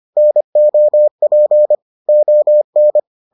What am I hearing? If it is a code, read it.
Morse code, heard as a keyed tone, about 25 wpm, 600 Hz. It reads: NOP ON